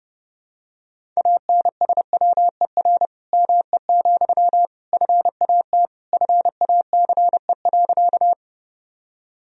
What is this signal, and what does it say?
Morse code, heard as a keyed tone, about 30 wpm, 700 Hz. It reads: ANSWER ME, FAT FACE.
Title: Sentences from Top 1000 Words - Review Part 4 - 30wpm